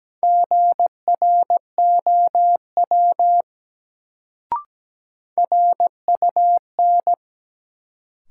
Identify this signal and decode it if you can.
Morse code, heard as a keyed tone, about 17 wpm, 700 Hz. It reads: GROW RUN
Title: Sentences from Top 300 Words - New Word Drill - 17wpm